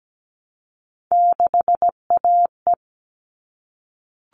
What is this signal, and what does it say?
Morse code, heard as a keyed tone, about 17 wpm, 700 Hz. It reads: 6AE
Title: All Cut Numbers - 3-Digit 17wpm